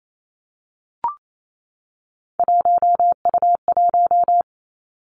Morse code, heard as a keyed tone, about 28 wpm, 700 Hz. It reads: 1U1
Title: Three Challenging Letter-Number Combinations 28wpm